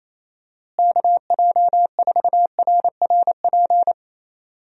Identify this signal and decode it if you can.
Morse code, heard as a keyed tone, about 28 wpm, 700 Hz. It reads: KJ4RRP
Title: USA Call Signs - 6 Characters - Repeated 2x - 28wpm